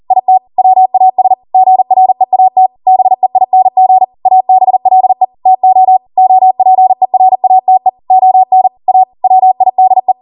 Morse code, heard as a keyed tone, about 40 wpm, 750 Hz. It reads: IT WAS GREAT BEING ABLE TO OPERATE ON A WIDE